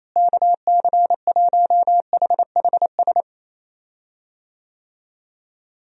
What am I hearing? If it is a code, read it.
Morse code, heard as a keyed tone, about 28 wpm, 700 Hz. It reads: KC1HHS
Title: USA Call Signs - 6 Characters - Repeated 3x - 28wpm